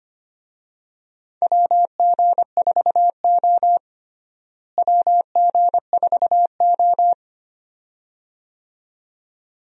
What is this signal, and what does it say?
Morse code, heard as a keyed tone, about 25 wpm, 700 Hz. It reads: WG4O WG4O
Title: USA Call Signs - 4 Characters - Repeated 2x - 25wpm